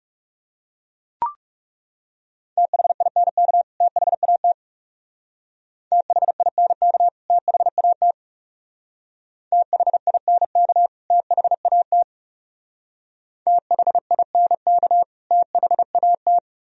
Morse code, heard as a keyed tone, about 35 wpm, 700 Hz. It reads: THINK THAT THINK THAT THINK THAT THINK THAT
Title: Speed-Racing - Sets of 2 Words 30wpm